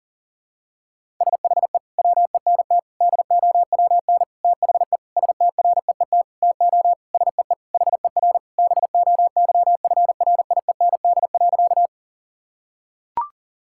Morse code, heard as a keyed tone, about 40 wpm, 700 Hz. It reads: SHE WENT DOWN THE STREET TO SEE HER BOYFRIEND.
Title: Sentences from Top 1000 Words - Review Part 5 - 40wpm